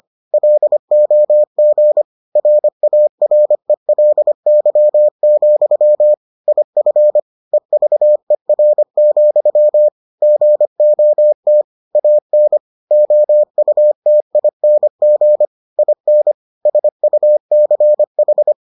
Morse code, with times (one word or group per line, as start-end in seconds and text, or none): 0.3-2.0 LOG
2.4-6.1 RARELY,
6.5-7.2 IF
7.5-9.9 EVER,
10.2-11.6 GOT
12.0-12.6 AN
12.9-15.5 OUTING
15.8-16.3 IN
16.7-18.5 SUCH